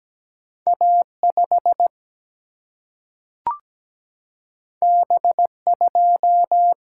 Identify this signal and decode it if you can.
Morse code, heard as a keyed tone, about 17 wpm, 700 Hz. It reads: A5 B2
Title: Single-Letter Single-Number 17wpm